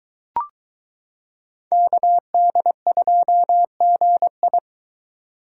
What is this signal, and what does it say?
Morse code, heard as a keyed tone, about 22 wpm, 700 Hz. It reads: KD2GI